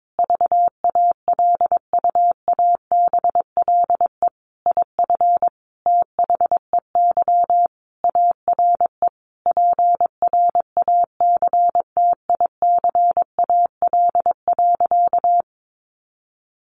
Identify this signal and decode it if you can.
Morse code, heard as a keyed tone, about 22 wpm, 700 Hz. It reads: VALUABLE IF THEY ARE PRACTICAL.
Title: Sentences from Top 1400 Words 22wpm